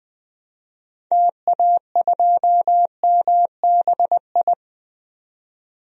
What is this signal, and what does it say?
Morse code, heard as a keyed tone, about 20 wpm, 700 Hz. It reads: TA2MBI
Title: DXCC-100 Call Signs - 6 Characters - 20wpm